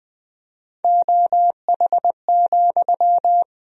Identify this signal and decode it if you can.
Morse code, heard as a keyed tone, about 20 wpm, 700 Hz. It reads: OH,
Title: 4x Word-Spacing - Sentences from Top 500 Words - 20wpm